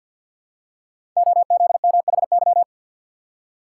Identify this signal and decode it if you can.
Morse code, heard as a keyed tone, about 48 wpm, 700 Hz. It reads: OZMHY
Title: Code Groups - 5 Letters - 50wpm